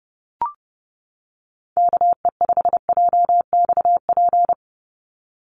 Morse code, heard as a keyed tone, about 30 wpm, 700 Hz. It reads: KE5JXP
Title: USA Call Signs 30wpm